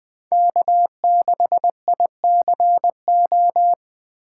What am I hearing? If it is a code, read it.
Morse code, heard as a keyed tone, about 20 wpm, 700 Hz. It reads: K6ICO
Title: USA Call Signs - 5 Characters - 2x Character Spacing - 20wpm